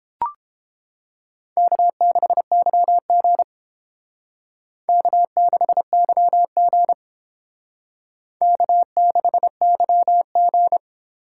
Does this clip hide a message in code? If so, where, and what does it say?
Yes: Morse code from 1.6 to 10.8 s, heard as a keyed tone, about 31 wpm, 700 Hz. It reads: K6YG K6YG K6YG